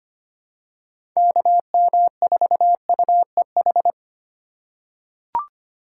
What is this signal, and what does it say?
Morse code, heard as a keyed tone, about 25 wpm, 700 Hz. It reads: KM4UEH
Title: USA Call Signs - 6 Characters - 4x Character Spacing - 25wpm